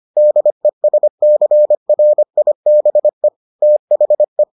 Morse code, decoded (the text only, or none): DESCRIBE THE